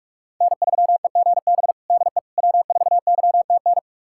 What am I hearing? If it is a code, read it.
Morse code, heard as a keyed tone, about 46 wpm, 700 Hz. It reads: N3EGB DE W4YTN
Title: USA Callsign DE Callsign - 5 Characters - 45wpm